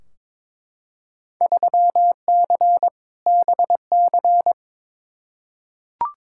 Morse code, heard as a keyed tone, about 22 wpm, 700 Hz. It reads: 3C BC
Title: Field Day Exchanges Encore - 22wpm